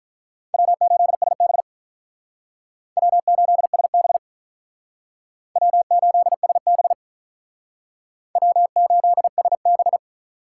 Morse code, heard as a keyed tone, about 45 wpm, 700 Hz. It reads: W8SB W8SB W8SB W8SB